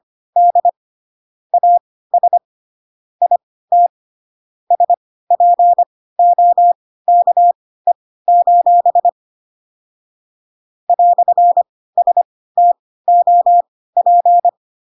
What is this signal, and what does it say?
Morse code, heard as a keyed tone, about 25 wpm, spaced as for 16 wpm, 700 Hz. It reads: D AS IT SPOKE: "STOP
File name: Pinocchio / Pinocchio ch1 10 25 ik1hge